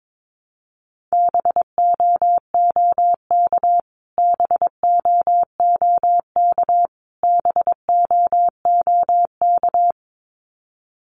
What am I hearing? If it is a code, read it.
Morse code, heard as a keyed tone, about 22 wpm, 700 Hz. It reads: BOOK BOOK BOOK
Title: Top 500 Words Repeated 22wpm